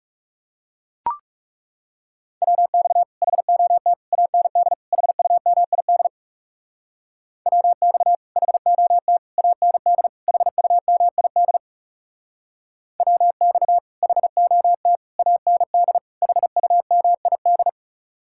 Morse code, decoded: WX HOT AND HUMID WX HOT AND HUMID WX HOT AND HUMID

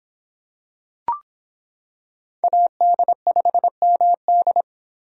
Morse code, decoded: AD5MD

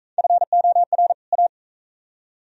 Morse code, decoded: FOR A

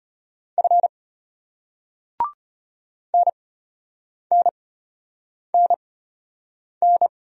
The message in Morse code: F NNNN

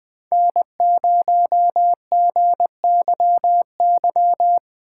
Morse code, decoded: N0GYY

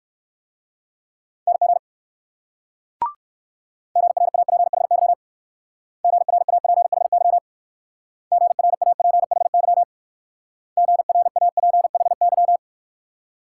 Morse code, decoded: NG GRAPHY GRAPHY GRAPHY GRAPHY

